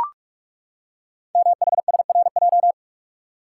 Morse code, heard as a keyed tone, about 44 wpm, 690 Hz. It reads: MHSRJ
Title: Code Groups - 5 Letters - 45wpm